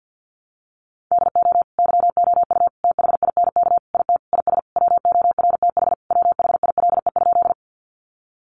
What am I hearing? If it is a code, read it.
Morse code, heard as a keyed tone, about 50 wpm, 700 Hz. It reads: DO YOU THINK IT IS WORTH WHILE?
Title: Sentences from Top 800 Words 50wpm